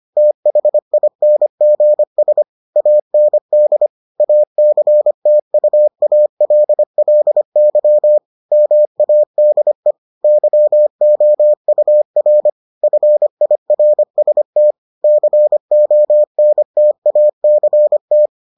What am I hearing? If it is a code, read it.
Morse code, heard as a keyed tone, about 25 wpm, 600 Hz. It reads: THINGS AND ACTUALLY MADE YOUR FIRST CONTACT